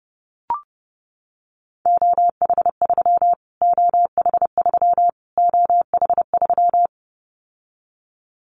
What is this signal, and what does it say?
Morse code, heard as a keyed tone, about 30 wpm, 700 Hz. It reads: OH3 OH3 OH3
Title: Callsign Prefixes for Top 25 DXCC - Repeated 3x - 30wpm